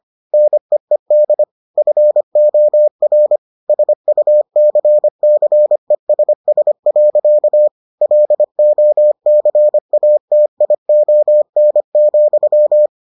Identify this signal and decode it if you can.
Morse code, heard as a keyed tone, about 25 wpm, 600 Hz. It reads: NEED FOR SUCCESS. LOCATION,